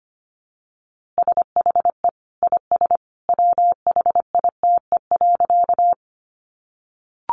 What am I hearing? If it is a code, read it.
Morse code, heard as a keyed tone, about 25 wpm, 700 Hz. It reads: SHE IS WHITE.
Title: Easy Phrases 25wpm